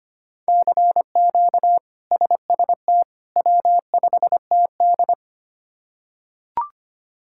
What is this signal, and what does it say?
Morse code, heard as a keyed tone, about 25 wpm, 700 Hz. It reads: CQ SST W5TD
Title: K1USN SST QSO Practice - 25wpm